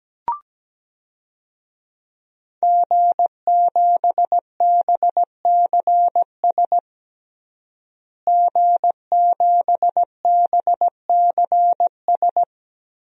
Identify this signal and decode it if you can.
Morse code, heard as a keyed tone, about 17 wpm, 700 Hz. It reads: G7BCS G7BCS
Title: DXCC-100 Call Signs - 5 Characters - Repeated 3x - 17wpm